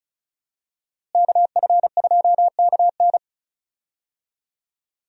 Morse code, heard as a keyed tone, about 36 wpm, 700 Hz. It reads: KF2KN
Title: USA Call Signs - 5 Characters - Repeated 2x - 35wpm